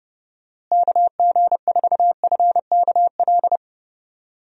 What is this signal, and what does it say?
Morse code, heard as a keyed tone, about 30 wpm, 700 Hz. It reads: KG4FKL
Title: USA Call Signs - 6 Characters - Repeated 2x - 30wpm